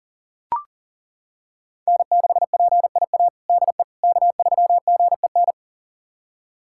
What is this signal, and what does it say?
Morse code, heard as a keyed tone, about 40 wpm, 690 Hz. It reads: N6PIA DE K3GEN
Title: USA Callsign DE Callsign - 5 Characters - 40wpm